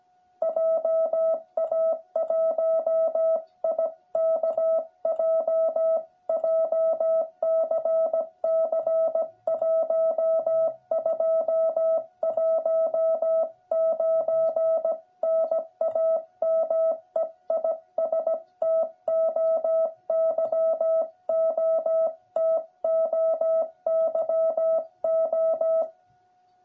Morse code, heard as a keyed tone, about 17 wpm, 660 Hz. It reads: JA1IKJ JCC 1219 NAME IS TOYO TOYO
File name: jh8cbh / 2021cw